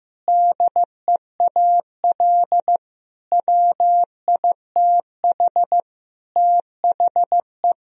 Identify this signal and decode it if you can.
Morse code, heard as a keyed tone, about 15 wpm, 700 Hz. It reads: DEAL WITH THE
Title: Speed-Racing - Sets of 4 Words 15wpm